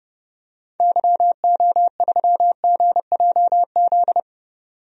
Morse code, heard as a keyed tone, about 30 wpm, 700 Hz. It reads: YO3GJZ
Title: DXCC-100 Call Signs - 6 Characters - 30wpm